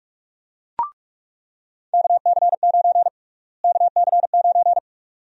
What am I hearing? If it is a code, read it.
Morse code, heard as a keyed tone, about 46 wpm, 700 Hz. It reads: KC9 KC9
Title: USA Call Sign Prefixes - Repeated 2x - 45wpm